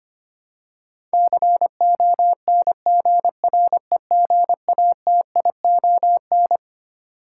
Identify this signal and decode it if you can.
Morse code, heard as a keyed tone, about 25 wpm, 700 Hz. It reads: CONGREGATION
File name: Twelve Letter Words - 25wpm